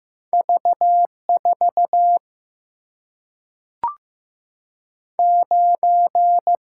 Morse code, heard as a keyed tone, about 15 wpm, 700 Hz. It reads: V 4 9
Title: Single-Letter Single-Number 15wpm